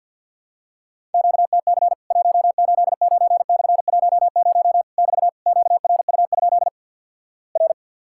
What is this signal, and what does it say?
Morse code, heard as a keyed tone, about 50 wpm, 700 Hz. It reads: QTC 189/10 <BT> QRU? R
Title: Worked All Europe DX Contest - QTC Exchange - 50wpm